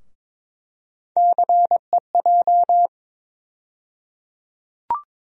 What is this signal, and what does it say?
Morse code, heard as a keyed tone, about 22 wpm, 700 Hz. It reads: CEJ
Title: USA Call Sign Postfixes - 3 Letters 22wpm